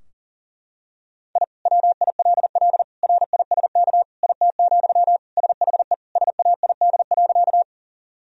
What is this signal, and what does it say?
Morse code, heard as a keyed tone, about 40 wpm, 700 Hz. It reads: I WILL RISK IT, SHE SAID.
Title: Sentences from Top 1000 Words - Review Part 4 - 40wpm